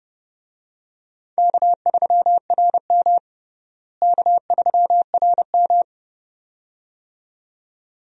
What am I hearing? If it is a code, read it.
Morse code, heard as a keyed tone, about 30 wpm, 700 Hz. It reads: K3RM K3RM
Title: USA Call Signs - 4 Characters - Repeated 2x - 30wpm